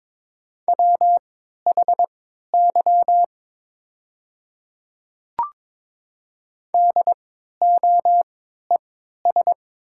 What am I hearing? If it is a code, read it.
Morse code, heard as a keyed tone, about 22 wpm, spaced as for 12 wpm, 700 Hz. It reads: WHY DOES
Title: 3x Character Spacing - Top 200 Words - 22wpm